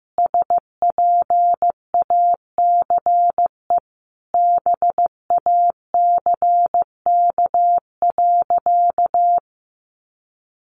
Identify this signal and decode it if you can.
Morse code, heard as a keyed tone, about 14 wpm, 700 Hz. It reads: SPACE BACK.
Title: Sentences from Top 1500 Words 15wpm